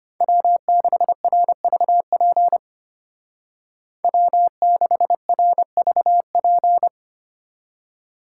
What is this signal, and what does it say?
Morse code, heard as a keyed tone, about 31 wpm, 700 Hz. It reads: W6RVP W6RVP
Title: Speed Racing - USA Call Signs 25wpm